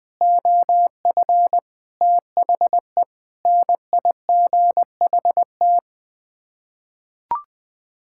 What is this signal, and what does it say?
Morse code, heard as a keyed tone, about 20 wpm, 700 Hz. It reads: OF THE NIGHT
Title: Sets of 8 Words - 20wpm